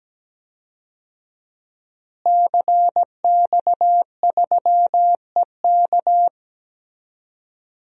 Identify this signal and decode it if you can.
Morse code, heard as a keyed tone, about 17 wpm, 700 Hz. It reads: CX3EK
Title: DXCC-100 Call Signs - 5 Characters - Repeated 2x - 17wpm